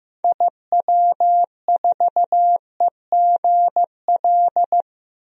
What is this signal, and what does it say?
Morse code, heard as a keyed tone, about 15 wpm, 700 Hz. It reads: IW4EGL
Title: DXCC-100 Call Signs - 6 Characters - 15wpm